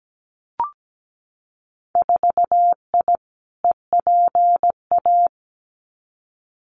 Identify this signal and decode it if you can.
Morse code, heard as a keyed tone, about 17 wpm, 700 Hz. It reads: 4I EPA
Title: Winter Field Day Exchanges Encore 17wpm